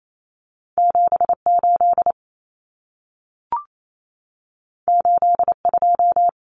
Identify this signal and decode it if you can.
Morse code, heard as a keyed tone, about 28 wpm, 700 Hz. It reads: 78 82
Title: Numbers - 2-Digit 28wpm